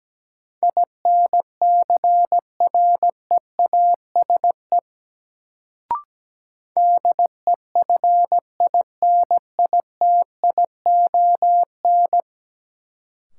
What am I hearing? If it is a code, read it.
Morse code, heard as a keyed tone, about 17 wpm, 700 Hz. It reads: INCREASE DEFINITION
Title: Sentences from Top 1000 Words - Review Part 4 - New Word Drill - 17wpm